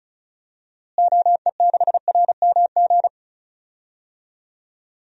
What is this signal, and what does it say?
Morse code, heard as a keyed tone, about 34 wpm, 700 Hz. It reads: OE6RMG
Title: DXCC-100 Call Signs - 6 Characters - Repeated 2x - 35wpm